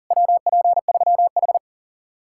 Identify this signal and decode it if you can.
Morse code, heard as a keyed tone, about 48 wpm, 700 Hz. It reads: WP3H
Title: Speed-Racing Common QSO Phrases Encore - 40wpm